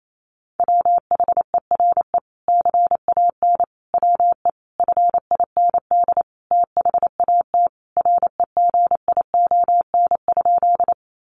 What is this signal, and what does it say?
Morse code, heard as a keyed tone, about 28 wpm, 700 Hz. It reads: WHERE CAN WE FIND THAT REGION?
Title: Sentences from Top 1500 Words 28wpm